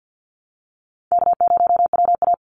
Morse code, heard as a keyed tone, about 50 wpm, 700 Hz. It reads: K0WA
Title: Call Signs Repeated 3x - 50wpm